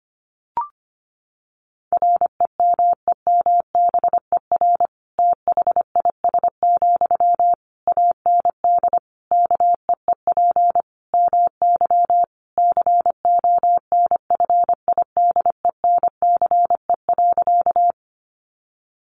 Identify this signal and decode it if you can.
Morse code, heard as a keyed tone, about 25 wpm, 700 Hz. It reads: REMEMBER THIS, AND KEEP MY CONFIDENCE.
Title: Sentences from Top 1500 Words 25wpm